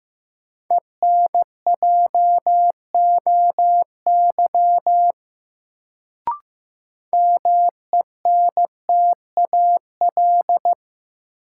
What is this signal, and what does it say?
Morse code, heard as a keyed tone, about 15 wpm, 700 Hz. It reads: ENJOY MENTAL
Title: Sentences from Top 1000 Words - Review Part 4 - New Word Drill - 15wpm